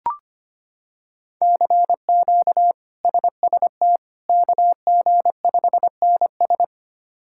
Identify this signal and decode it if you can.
Morse code, heard as a keyed tone, about 25 wpm, 700 Hz. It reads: CQ SST KG5NS
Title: K1USN SST QSO Practice - 25wpm